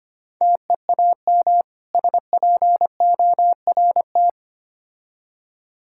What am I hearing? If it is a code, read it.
Morse code, heard as a keyed tone, about 29 wpm, 690 Hz. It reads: TEAM SPORT
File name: practice / 25wpm / Difficult Words - Speed-Racing with Hints - 25wpm